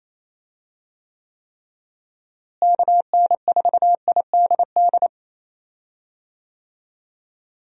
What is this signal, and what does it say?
Morse code, heard as a keyed tone, about 28 wpm, 700 Hz. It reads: KN4IDD
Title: USA Call Signs - 6 Characters - Repeated 3x - 28wpm